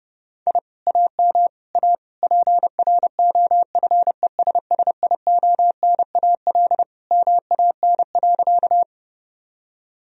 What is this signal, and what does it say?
Morse code, heard as a keyed tone, about 32 wpm, 700 Hz. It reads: I AM A PROFESSIONAL MAN.
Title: Sentences from Top 1000 Words - Review Part 4 - 30wpm